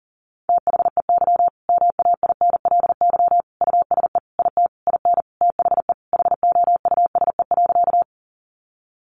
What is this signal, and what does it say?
Morse code, heard as a keyed tone, about 40 wpm, 700 Hz. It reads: THEY MAINLY USE IT IN THE HOUSE.
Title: Sentences from Top 1400 Words 40wpm